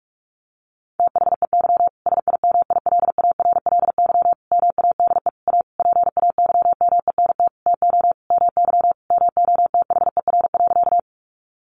Morse code, heard as a keyed tone, about 44 wpm, 700 Hz. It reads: THEY SIMILARLY MADE A PAYMENT TO MY MOTHER.
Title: Sentences from Top 1400 Words 45wpm